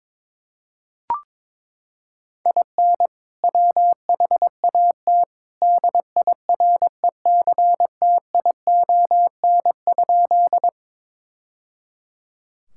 Morse code, heard as a keyed tone, about 22 wpm, 700 Hz. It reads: IN WHAT DIRECTION?
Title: Sentences from Top 1000 Words - Review Part 5 - 22wpm